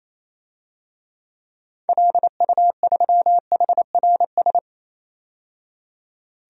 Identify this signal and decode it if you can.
Morse code, heard as a keyed tone, about 28 wpm, 700 Hz. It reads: LU3HRS
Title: DXCC-100 Call Signs - 6 Characters - 28wpm